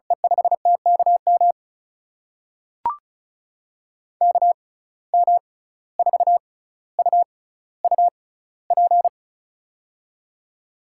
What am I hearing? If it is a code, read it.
Morse code, heard as a keyed tone, about 36 wpm, spaced as for 17 wpm, 700 Hz. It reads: E5TKM KM4UUP
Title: USA Call Signs - 6 Characters - 6x Character Spacing - 35wpm